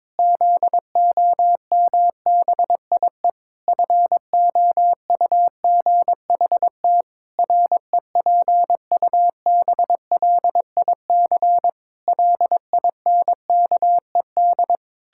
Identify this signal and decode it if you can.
Morse code, heard as a keyed tone, about 22 wpm, 700 Hz. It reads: ZOMBIE FOUGHT REPUBLIC LINKED
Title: Pangram Challenge v2 - 22wpm